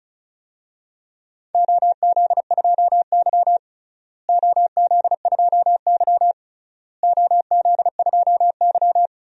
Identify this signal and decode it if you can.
Morse code, heard as a keyed tone, about 36 wpm, 700 Hz. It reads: OZ2Y OZ2Y OZ2Y
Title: DXCC-100 Call Signs - 4 Characters - Repeated 3x - 35wpm